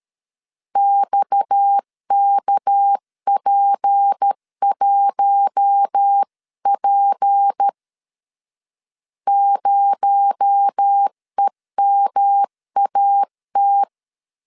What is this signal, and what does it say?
Morse code, heard as a keyed tone, about 13 wpm, 800 Hz. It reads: XKP1P 0EMAT
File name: LEZIONI / 7 GF9Z / 7 GF 12wpm 2122